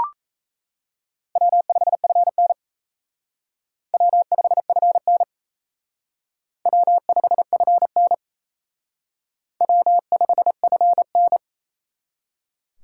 Morse code, heard as a keyed tone, about 35 wpm, 700 Hz. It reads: W5FN W5FN W5FN W5FN
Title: Speed Racing - USA 1x2 Call Signs 28wpm